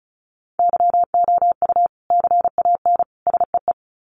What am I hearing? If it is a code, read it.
Morse code, heard as a keyed tone, about 34 wpm, 700 Hz. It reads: YOU CAN SEE
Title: Sets of 3 Words 35wpm